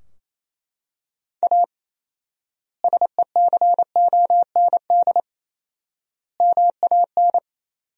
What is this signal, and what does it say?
Morse code, heard as a keyed tone, about 27 wpm, 700 Hz. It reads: A SECOND MAN
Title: 4x Word-Spacing - Sentences from Top 300 Words - 28wpm